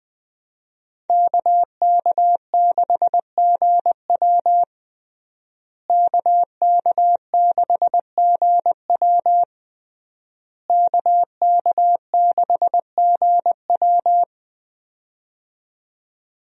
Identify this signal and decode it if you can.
Morse code, heard as a keyed tone, about 20 wpm, 700 Hz. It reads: KK6GW KK6GW KK6GW
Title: USA Call Signs - 5 Characters - Repeated 3x - 20wpm